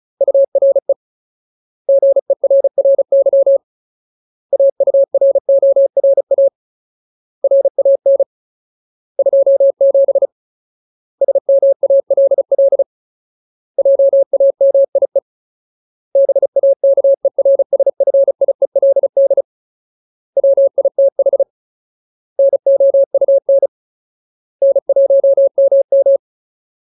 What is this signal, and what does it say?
Morse code, heard as a keyed tone, about 34 wpm, 550 Hz. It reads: URE GERRY AURORA RAN 27 SMALL JAMIE BAKERSFIELD WITH NOUN N1MM
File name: Ultimate Word List / 35WPM Ultimate